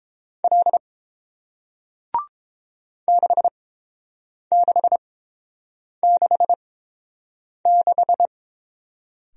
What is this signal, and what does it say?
Morse code, heard as a keyed tone, about 30 wpm, spaced as for 7 wpm, 700 Hz. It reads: L 6666